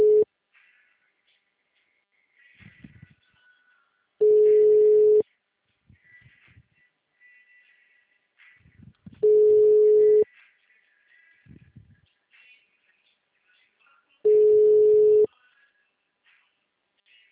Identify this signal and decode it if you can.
Ringback tone (Europe, 425 Hz).